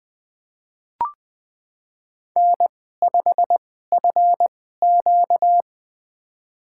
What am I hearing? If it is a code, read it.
Morse code, heard as a keyed tone, about 20 wpm, spaced as for 14 wpm, 700 Hz. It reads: N5FQ
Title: USA Call Signs - 2x Character Spacing - 20wpm